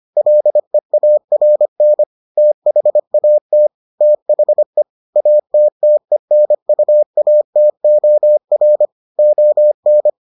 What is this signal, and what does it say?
Morse code, heard as a keyed tone, about 25 wpm, 600 Hz. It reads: LEARN THAT THE ATTENUATOR ON